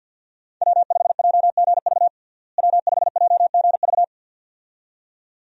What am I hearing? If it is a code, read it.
Morse code, heard as a keyed tone, about 50 wpm, 700 Hz. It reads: W5JGV W5JGV
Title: USA Call Signs - 5 Characters - Repeated 2x - 50wpm